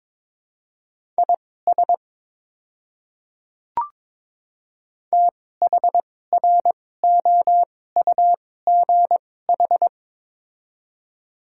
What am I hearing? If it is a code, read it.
Morse code, heard as a keyed tone, about 22 wpm, spaced as for 16 wpm, 700 Hz. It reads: IS THROUGH
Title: 2x Character Spacing - Top 200 Words - 22wpm